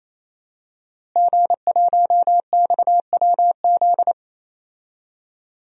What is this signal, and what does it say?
Morse code, heard as a keyed tone, about 28 wpm, 700 Hz. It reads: G1XWZ
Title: DXCC-100 Call Signs - 5 Characters - Repeated 3x - 28wpm